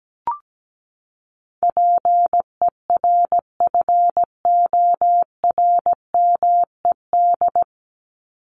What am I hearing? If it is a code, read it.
Morse code, heard as a keyed tone, about 17 wpm, 700 Hz. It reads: PERFORMED
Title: ED Suffix - Top 100 Words with Base Words 17wpm